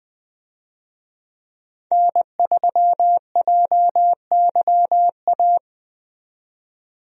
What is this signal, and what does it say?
Morse code, heard as a keyed tone, about 20 wpm, 700 Hz. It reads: N3JYA
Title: USA Call Signs - 5 Characters - 20wpm